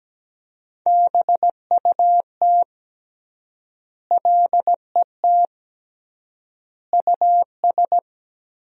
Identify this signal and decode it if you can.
Morse code, heard as a keyed tone, about 17 wpm, 700 Hz. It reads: BUT LET US
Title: 3x Word-Spacing - Sentences from Top 300 Words - 17wpm